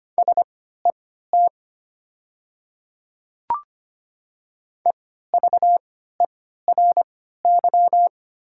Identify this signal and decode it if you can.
Morse code, heard as a keyed tone, about 25 wpm, spaced as for 14 wpm, 690 Hz. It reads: SET EVERY
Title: 3x Character Spacing - Top 200 Words - 25wpm